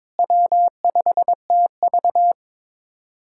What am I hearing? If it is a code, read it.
Morse code, heard as a keyed tone, about 22 wpm, 700 Hz. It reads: W5TV
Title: USA Callsign DE Callsign - 4 Characters - 22wpm